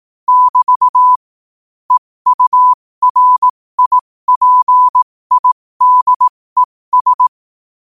Morse code, heard as a keyed tone, about 18 wpm, spaced as for 15 wpm, 1000 Hz. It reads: <BT> EURIPIDES
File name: qotdfiles / 15wpm